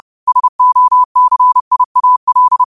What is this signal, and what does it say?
Morse code, heard as a keyed tone, about 30 wpm, 1000 Hz. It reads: SOCIAL